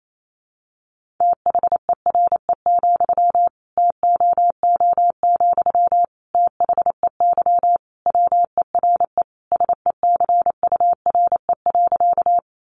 Morse code, heard as a keyed tone, about 28 wpm, 700 Hz. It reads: THERE, TOO, THEY WERE SECURE.
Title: Sentences from Top 1200 Words 28wpm